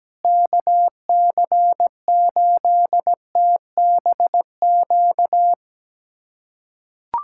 Morse code, heard as a keyed tone, about 17 wpm, 700 Hz. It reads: KC8TBQ
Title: USA Callsign DE Callsign 17wpm